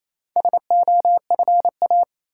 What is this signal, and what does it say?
Morse code, heard as a keyed tone, about 27 wpm, 700 Hz. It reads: SOFA
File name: Easy Words - Speed-Racing with Hints - 22wpm